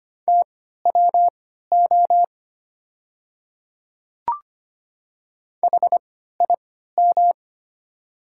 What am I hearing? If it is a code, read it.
Morse code, heard as a keyed tone, about 25 wpm, spaced as for 14 wpm, 700 Hz. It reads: TWO HIM
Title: 3x Character Spacing - Top 300 Words - 25wpm